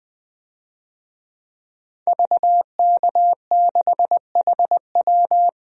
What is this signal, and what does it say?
Morse code, heard as a keyed tone, about 20 wpm, 700 Hz. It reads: VK6HW
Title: DXCC-100 Call Signs - 5 Characters - Repeated 2x - 20wpm